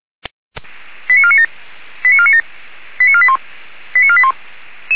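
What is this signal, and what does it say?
Five-tone selective call (ZVEI), heard as quick runs of five tones, tones of about 70 ms. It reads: 89497 89497 89481 89481
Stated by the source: ZVEI five-tone selective call 85749 received by SDR, logged by feuerwehr-orlamuende fire brigade server.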